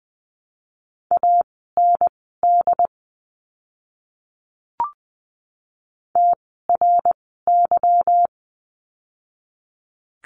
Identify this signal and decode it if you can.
Morse code, heard as a keyed tone, about 20 wpm, spaced as for 15 wpm, 700 Hz. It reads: AND TRY
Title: 160 Lesson 26f - Letter g - Words 20wpm